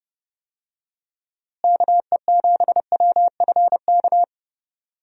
Morse code, heard as a keyed tone, about 30 wpm, 700 Hz. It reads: KE7WFK
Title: USA Call Signs - 6 Characters - Repeated 2x - 30wpm